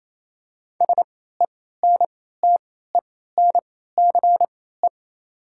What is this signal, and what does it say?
Morse code, heard as a keyed tone, about 27 wpm, spaced as for 16 wpm, 700 Hz. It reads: SENTENCE
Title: 3x Character Spacing - Top 300 Words - 28wpm